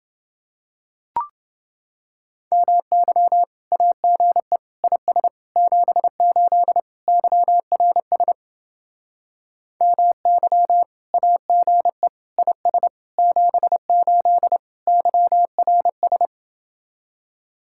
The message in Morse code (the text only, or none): MY AGE IS 78 YRS MY AGE IS 78 YRS